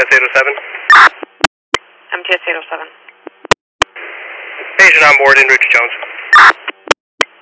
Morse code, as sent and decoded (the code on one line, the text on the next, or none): none